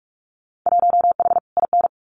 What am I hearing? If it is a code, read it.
Morse code, heard as a keyed tone, about 46 wpm, 700 Hz. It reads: 1H IN
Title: Winter Field Day Exchanges 45wpm